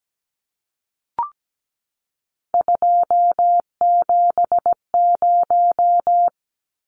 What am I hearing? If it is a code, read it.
Morse code, heard as a keyed tone, about 17 wpm, 700 Hz. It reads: 270
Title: Numbers - 3-Digit 17wpm